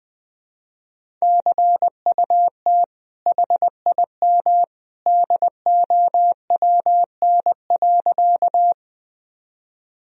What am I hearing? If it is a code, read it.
Morse code, heard as a keyed tone, about 20 wpm, 700 Hz. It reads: CUT HIM DOWN.
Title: Sentences from Top 1000 Words - Review Part 1 - 20wpm